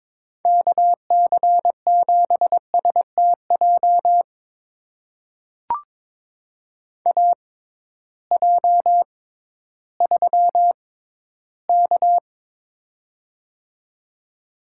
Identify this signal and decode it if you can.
Morse code, heard as a keyed tone, about 22 wpm, spaced as for 13 wpm, 700 Hz. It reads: KC7STJ AJ3K
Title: USA Call Signs - 6x Character Spacing - 22wpm